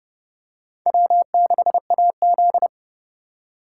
Morse code, heard as a keyed tone, about 28 wpm, 700 Hz. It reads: W6AZ